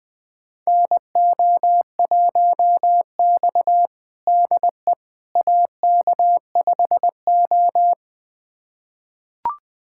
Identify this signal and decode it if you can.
Morse code, heard as a keyed tone, about 20 wpm, 700 Hz. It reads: NO1X DE AK5O
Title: USA Callsign DE Callsign - 4 Characters - 20wpm